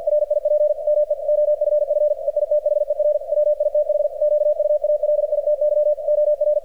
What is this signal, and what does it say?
Morse code, heard as a keyed tone, about 48 wpm, 600 Hz, with radio noise. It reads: RIP ME OFF EITHER WITH QUALITY OR
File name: archived / CRSnet